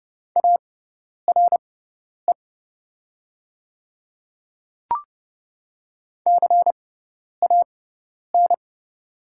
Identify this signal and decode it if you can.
Morse code, heard as a keyed tone, about 31 wpm, spaced as for 10 wpm, 700 Hz. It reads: ARE CAN
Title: 6x Character Spacing - Top 200 Words - 30wpm